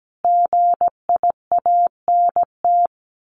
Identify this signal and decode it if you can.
Morse code, heard as a keyed tone, about 17 wpm, 700 Hz. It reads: GIANT
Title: A Sound Of Thunder 17wpm